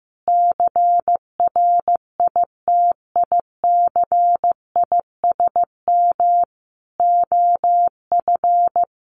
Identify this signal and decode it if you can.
Morse code, heard as a keyed tone, about 15 wpm, 700 Hz. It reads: CRITICISM OF
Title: Sentences from Top 1300 Words 15wpm